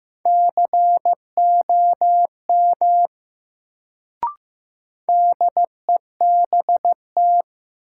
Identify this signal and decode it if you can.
Morse code, heard as a keyed tone, about 15 wpm, 700 Hz. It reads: COM DEBT
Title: Sentences from Top 1300 Words - New Word Drill Alt - 15wpm